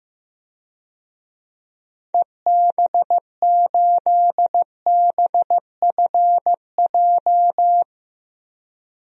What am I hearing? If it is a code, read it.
Morse code, heard as a keyed tone, about 15 wpm, 700 Hz. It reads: EB8BFJ
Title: DXCC-100 Call Signs - 6 Characters - 15wpm